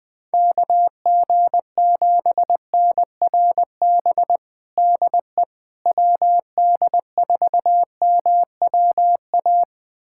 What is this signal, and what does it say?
Morse code, heard as a keyed tone, about 20 wpm, 700 Hz. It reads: KG7NRB DE WD4MWA
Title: USA Callsign DE Callsign 20wpm